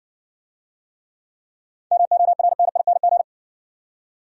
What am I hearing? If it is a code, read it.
Morse code, heard as a keyed tone, about 60 wpm, 700 Hz. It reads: MORNING